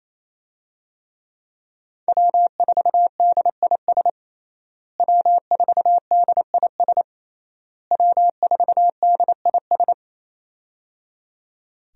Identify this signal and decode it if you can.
Morse code, heard as a keyed tone, about 27 wpm, 700 Hz. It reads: W4DIS W4DIS W4DIS
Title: USA Call Signs - 5 Characters - Repeated 3x - 28wpm